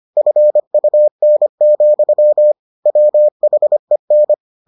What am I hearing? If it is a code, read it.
Morse code, heard as a keyed tone, about 25 wpm, 600 Hz. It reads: FUN, WHEN